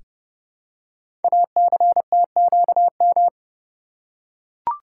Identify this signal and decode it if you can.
Morse code, heard as a keyed tone, about 30 wpm, 700 Hz. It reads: ACTQM
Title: Code Groups - 5 Letters - 30wpm